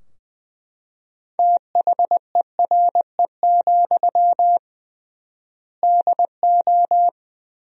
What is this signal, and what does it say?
Morse code, heard as a keyed tone, about 20 wpm, 700 Hz. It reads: THERE, DO
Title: 3x Word-Spacing - Sentences from Top 300 Words - 20wpm